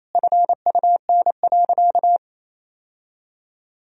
Morse code, heard as a keyed tone, about 28 wpm, 700 Hz. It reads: FUN.
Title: Sentences from Top 1000 Words - Review Part 4 - 28wpm